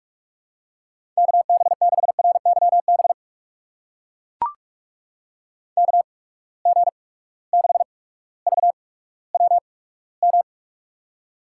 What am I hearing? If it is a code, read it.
Morse code, heard as a keyed tone, about 42 wpm, spaced as for 19 wpm, 700 Hz. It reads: KB6RYB KG6VWM
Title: USA Call Signs - 6 Characters - 8x Character Spacing - 45wpm